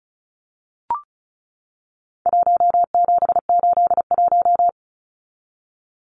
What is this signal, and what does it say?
Morse code, heard as a keyed tone, about 34 wpm, 700 Hz. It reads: 1781
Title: Numbers - Contest Style - 4-Digit 35wpm